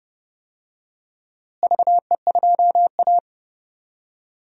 Morse code, heard as a keyed tone, about 31 wpm, 700 Hz. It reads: VE2A
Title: DXCC-100 Call Signs - 4 Characters - Repeated 3x - 30wpm